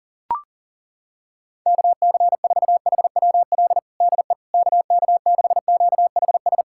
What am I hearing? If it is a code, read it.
Morse code, heard as a keyed tone, about 40 wpm, 700 Hz. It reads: KC4HWL DE KK6QHS
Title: USA Callsign DE Callsign 40wpm